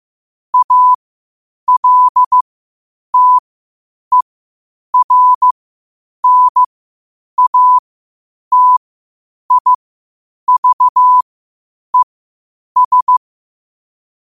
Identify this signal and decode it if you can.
Morse code, heard as a keyed tone, about 15 wpm, spaced as for 8 wpm, 1000 Hz. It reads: ALTERNATIVES
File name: qotdfiles / 8wpm